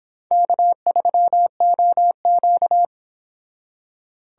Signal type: Morse code

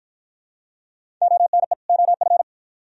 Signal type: Morse code

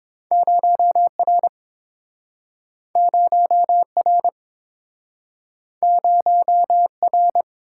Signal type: Morse code

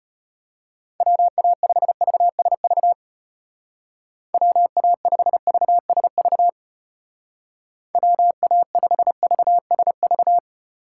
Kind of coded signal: Morse code

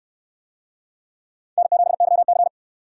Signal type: Morse code

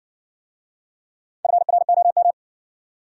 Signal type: Morse code

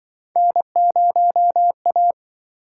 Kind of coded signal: Morse code

